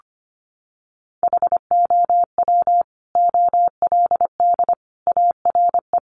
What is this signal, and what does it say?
Morse code, heard as a keyed tone, about 25 wpm, 700 Hz. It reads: HOW OLD ARE
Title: Sentences from Top 200 Words 25wpm